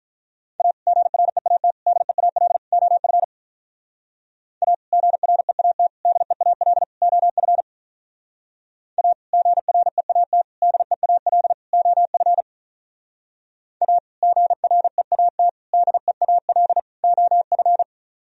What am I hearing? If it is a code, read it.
Morse code, heard as a keyed tone, about 44 wpm, 700 Hz. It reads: A GREAT DEAL OF A GREAT DEAL OF A GREAT DEAL OF A GREAT DEAL OF